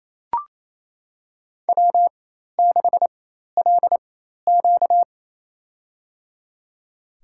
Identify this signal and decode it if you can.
Morse code, heard as a keyed tone, about 27 wpm, spaced as for 13 wpm, 700 Hz. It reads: W6LQ
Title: USA Call Signs - 4 Characters - 4x Character Spacing - 28wpm